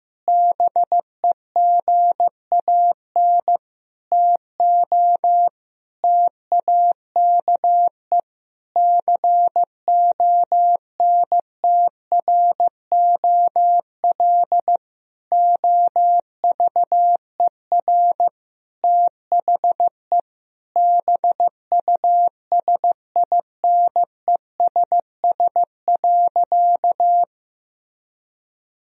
Morse code, heard as a keyed tone, about 15 wpm, 700 Hz. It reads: BEGAN TO TAKE CONTROL OVER THE BUSINESS.
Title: Sentences from Top 1000 Words - Review Part 3 - 15wpm